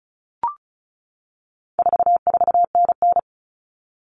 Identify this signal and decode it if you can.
Morse code, heard as a keyed tone, about 34 wpm, 700 Hz. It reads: 44NN
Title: Common Cut Numbers - 4-Digit 35wpm